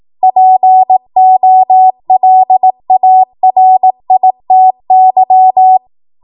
Morse code, heard as a keyed tone, about 18 wpm, 750 Hz. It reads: POLARITY
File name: Morse / 18WPM